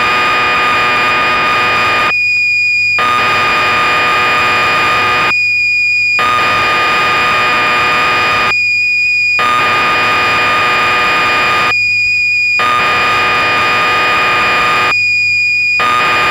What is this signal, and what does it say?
Cassette data (Kansas City type).
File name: magazines / Acorn User / 1984-10 BBC Programs